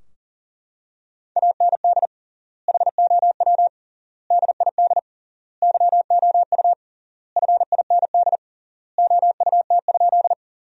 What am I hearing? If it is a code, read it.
Morse code, heard as a keyed tone, about 40 wpm, 700 Hz. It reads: AND HOW DID YOU FIND OUT?
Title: 3x Word-Spacing - Sentences from Top 100 Words - 40wpm